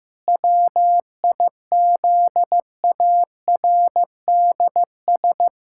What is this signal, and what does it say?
Morse code, heard as a keyed tone, about 15 wpm, 700 Hz. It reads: WIZARDS
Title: Pangram Challenge v2 - 15wpm